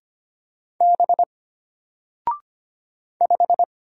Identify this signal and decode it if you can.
Morse code, heard as a keyed tone, about 25 wpm, 700 Hz. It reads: B 5
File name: Single-Letter-Number-25wpm